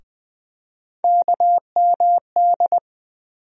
Morse code, heard as a keyed tone, about 20 wpm, 700 Hz. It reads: KMD
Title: USA Call Sign Postfixes - 3 Letters 20wpm